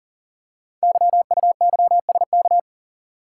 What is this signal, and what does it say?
Morse code, heard as a keyed tone, about 39 wpm, 700 Hz. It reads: YUYSK
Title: Code Groups - 5 Letters - 40wpm